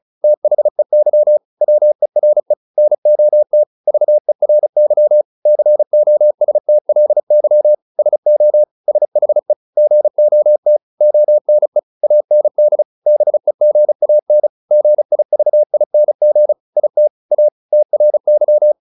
Morse code, heard as a keyed tone, about 34 wpm, 600 Hz. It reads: THEY WERE NOT VERY COSTLY SO SHE GOT ONE AND BEGAN GIVING IT A TRY
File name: SS303 35